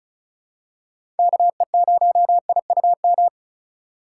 Morse code, heard as a keyed tone, about 36 wpm, 700 Hz. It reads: KE0IUM
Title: USA Call Signs - 6 Characters - 3x Character Spacing - 35wpm